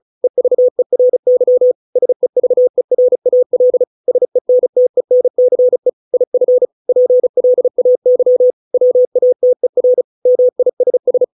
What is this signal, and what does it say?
Morse code, heard as a keyed tone, about 34 wpm, 500 Hz. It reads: EVERY SEVERAL SENTENCE IF PLAY WATER MISS